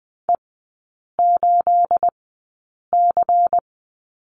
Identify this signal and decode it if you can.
Morse code, heard as a keyed tone, about 20 wpm, spaced as for 8 wpm, 690 Hz. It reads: E8C
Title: 212 Lesson 34e - Number 8 - Three Characters 20wpm